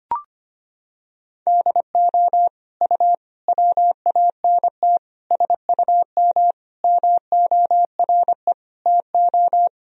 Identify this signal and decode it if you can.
Morse code, heard as a keyed tone, about 25 wpm, 700 Hz. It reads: DO U WANT SUM MORE TO
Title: Sentences from Top 300 Words and Abbreviations 25wpm